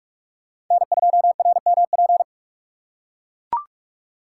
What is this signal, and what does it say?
Morse code, heard as a keyed tone, about 42 wpm, 700 Hz. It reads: N2RMP
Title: USA Call Signs - 5 Characters - 4x Character Spacing - 45wpm